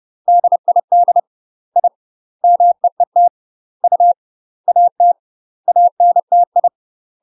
Morse code, heard as a keyed tone, about 30 wpm, 700 Hz. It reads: DID I MEET U AT ANTI
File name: UK MASTER QSOs TRAINING / 30WPM / UK MASTER QSOs 0017